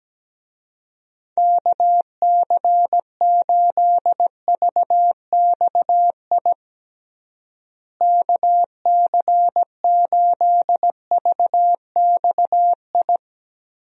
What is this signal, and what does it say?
Morse code, heard as a keyed tone, about 17 wpm, 700 Hz. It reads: KC8VXI KC8VXI
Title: USA Call Signs - 6 Characters - Repeated 3x - 17wpm